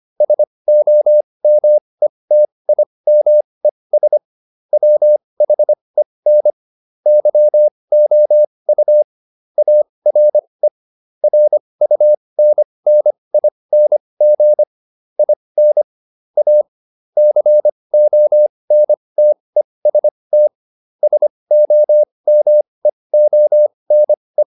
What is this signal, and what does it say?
Morse code, heard as a keyed tone, about 25 wpm, spaced as for 20 wpm, 600 Hz. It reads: SOMETIMES WHEN YOU ARE RUNNING IN A CONTEST SOMEONE
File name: SS209 20